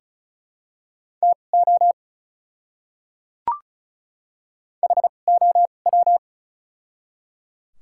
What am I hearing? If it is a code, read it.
Morse code, heard as a keyed tone, about 36 wpm, spaced as for 25 wpm, 700 Hz. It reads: TO HOW
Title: 2x Character Spacing - Top 100 Words - 35wpm